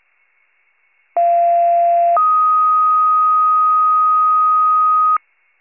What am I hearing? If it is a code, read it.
Two-tone page, heard as a one-second tone, then a three-second tone, about 690 then 1250 Hz. It is heard from 1.2 to 5.2 s.